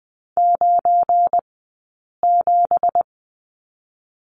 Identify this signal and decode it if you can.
Morse code, heard as a keyed tone, about 20 wpm, 700 Hz. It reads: 9 7
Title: 183 Lesson 30d - Letter b - Two Characters 20wpm